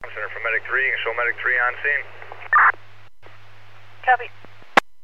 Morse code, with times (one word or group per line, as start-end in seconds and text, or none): none